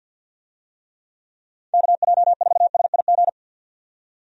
Morse code, heard as a keyed tone, about 51 wpm, 700 Hz. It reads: KJ4SIG
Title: USA Call Signs - 6 Characters - 50wpm